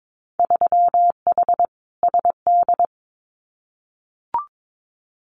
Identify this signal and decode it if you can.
Morse code, heard as a keyed tone, about 22 wpm, 700 Hz. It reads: 3H SD
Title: Winter Field Day Exchanges Encore 22wpm